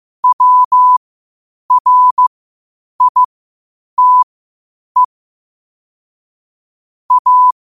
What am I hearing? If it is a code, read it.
Morse code, heard as a keyed tone, about 15 wpm, spaced as for 8 wpm, 1000 Hz. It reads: WRITE A
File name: qotdfiles / 8wpm